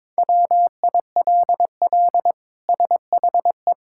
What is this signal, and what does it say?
Morse code, heard as a keyed tone, about 22 wpm, 700 Hz. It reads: WILL SHE
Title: Sentences from Top 1600 Words 22wpm